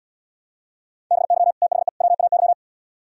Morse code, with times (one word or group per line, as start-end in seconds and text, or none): 1.1-1.5 BY
1.6-1.9 THE
2.0-2.5 WAY